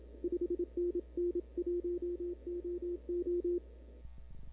Morse code, heard as a keyed tone, about 26 wpm, 340 Hz. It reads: 5NN 1OO
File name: CW / 2005-k5tr arrl cw dx -00:00